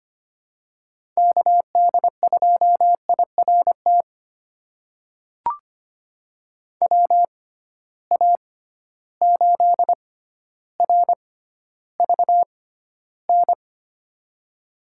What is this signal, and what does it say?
Morse code, heard as a keyed tone, about 25 wpm, spaced as for 13 wpm, 700 Hz. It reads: KD2IRT WA8RVN
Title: USA Call Signs - 6 Characters - 6x Character Spacing - 25wpm